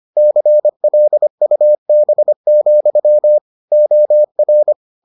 Morse code, heard as a keyed tone, about 25 wpm, 600 Hz. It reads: CLUB, OR